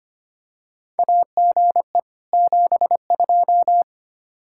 Morse code, heard as a keyed tone, about 25 wpm, 700 Hz. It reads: AGE 72
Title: Practice QSO - Exchange 3 - More Info - 25wpm